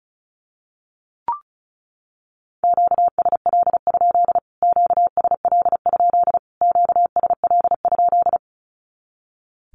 Morse code, heard as a keyed tone, about 34 wpm, 700 Hz. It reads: QSL? QSL? QSL?
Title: Q-Codes Repeated 3x 35wpm